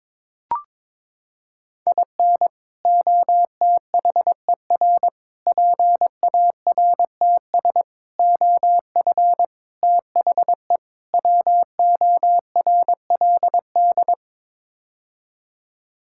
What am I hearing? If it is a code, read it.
Morse code, heard as a keyed tone, about 23 wpm, 700 Hz. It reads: IN OTHER PARTS OF THE WORLD
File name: practice / 22wpm / Sets of 6 Words - 22wpm